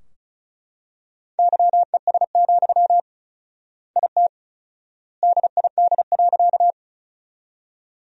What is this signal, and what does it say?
Morse code, heard as a keyed tone, about 36 wpm, 700 Hz. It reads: YES, IT DID.
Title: 4x Word-Spacing - Sentences from Top 500 Words - 35wpm